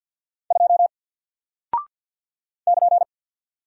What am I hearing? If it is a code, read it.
Morse code, heard as a keyed tone, about 49 wpm, 700 Hz. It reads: 2 <KN>
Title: Single Character (Letter:Number:Punctuation:Prosign) Alt 50wpm